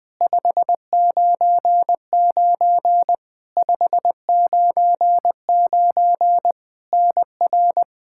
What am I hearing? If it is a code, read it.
Morse code, heard as a keyed tone, about 20 wpm, 700 Hz. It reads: 599 599 NR
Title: K1USN SST QSO Practice 20wpm